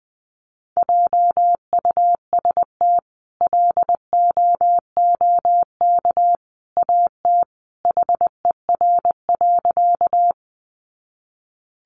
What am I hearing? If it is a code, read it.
Morse code, heard as a keyed tone, about 20 wpm, 700 Hz. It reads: JUST LOOK AT HER.
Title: Sentences from Top 200 Words 20wpm